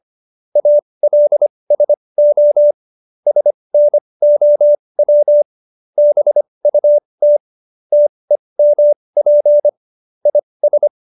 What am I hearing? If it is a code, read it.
Morse code, heard as a keyed tone, about 25 wpm, spaced as for 20 wpm, 590 Hz. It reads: ALSO SNOW BUT TEMP IS
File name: QSO321 20